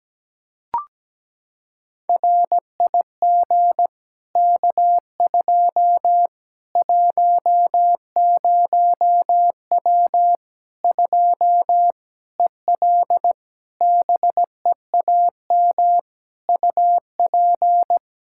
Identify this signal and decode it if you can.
Morse code, heard as a keyed tone, about 17 wpm, 700 Hz. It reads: RIG K2 10W 2 EL BEAM UP
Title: Practice QSO - Exchange 2 - Info - 17wpm